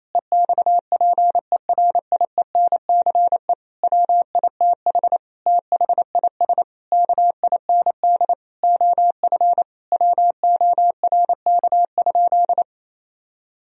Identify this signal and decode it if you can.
Morse code, heard as a keyed tone, about 27 wpm, 700 Hz. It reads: EXPERIENCE WITH THIS KIND OF WORK?
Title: Sentences from Top 1000 Words - Review Part 3 - 28wpm